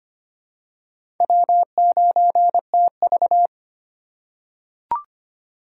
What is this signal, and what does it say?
Morse code, heard as a keyed tone, about 25 wpm, 700 Hz. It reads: W9TV
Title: USA Call Signs - 4 Characters - 2x Character Spacing - 25wpm